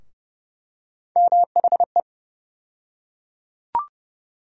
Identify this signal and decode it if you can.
Morse code, heard as a keyed tone, about 30 wpm, 700 Hz. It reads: MHE
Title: USA Call Sign Postfixes - 3 Letters 30wpm